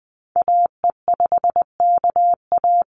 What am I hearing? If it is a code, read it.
Morse code, heard as a keyed tone, about 20 wpm, 700 Hz. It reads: AE5KA
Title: NA SOTA Activator Callsigns 20wpm